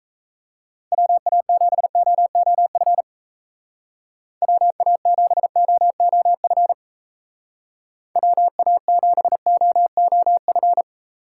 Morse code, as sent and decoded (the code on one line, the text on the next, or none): .-- .- --... --- --- ..-. / .-- .- --... --- --- ..-. / .-- .- --... --- --- ..-.
WA7OOF WA7OOF WA7OOF